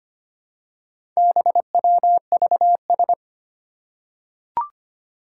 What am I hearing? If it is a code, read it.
Morse code, heard as a keyed tone, about 25 wpm, 700 Hz. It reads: BWVS
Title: Four Challenging Letter-Number Combinations 25wpm